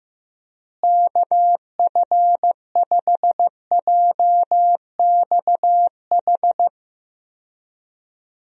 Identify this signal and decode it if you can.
Morse code, heard as a keyed tone, about 15 wpm, 700 Hz. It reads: KF5JXH
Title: USA Call Signs - 6 Characters - Repeated 2x - 15wpm